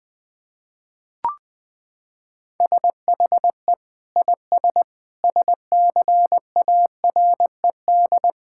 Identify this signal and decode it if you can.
Morse code, heard as a keyed tone, about 20 wpm, 700 Hz. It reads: SHE IS SCARED
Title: Sentences from Top 1000 Words - Review Part 6 - 20wpm